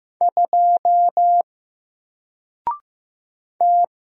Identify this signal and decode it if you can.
Morse code, heard as a keyed tone, about 15 wpm, 700 Hz. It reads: 2 T
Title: Single Character (Letter:Number:Punctuation:Prosign) Alt 15wpm